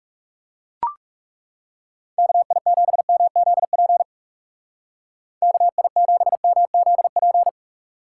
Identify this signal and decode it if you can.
Morse code, heard as a keyed tone, about 36 wpm, 700 Hz. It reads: KI7MZP KI7MZP
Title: Speed Racing - USA 2x3 Call Signs 30wpm